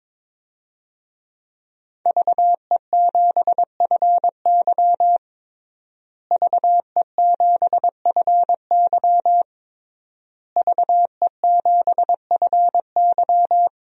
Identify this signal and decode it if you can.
Morse code, heard as a keyed tone, about 21 wpm, 700 Hz. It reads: VE7FY VE7FY VE7FY
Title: DXCC-100 Call Signs - 5 Characters - Repeated 3x - 22wpm